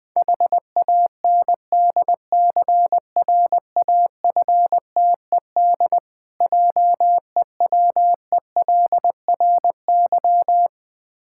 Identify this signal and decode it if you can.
Morse code, heard as a keyed tone, about 20 wpm, 690 Hz. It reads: HANDCRAFTED JEWELRY
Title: Difficult Words with Hints - 20wpm